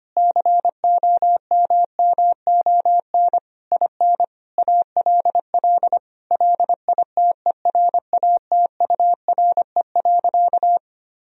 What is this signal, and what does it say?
Morse code, heard as a keyed tone, about 25 wpm, 700 Hz. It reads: COMMON IN ALL LITERATURE.
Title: Sentences from Top 1000 Words - Review Part 3 - 25wpm